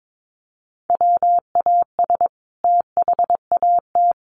Morse code, heard as a keyed tone, about 22 wpm, 700 Hz. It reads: WAS THAT
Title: Sentences from Top 600 Words 22wpm